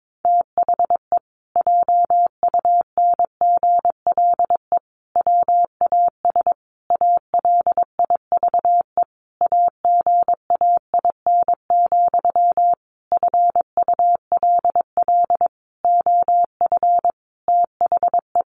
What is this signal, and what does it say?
Morse code, heard as a keyed tone, about 22 wpm, 700 Hz. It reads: THE JUNGLE WAS ALIVE AGAIN, FULL OF THE
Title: A Sound Of Thunder 22wpm